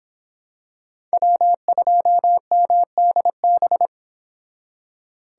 Morse code, heard as a keyed tone, about 26 wpm, 700 Hz. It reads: W2MDB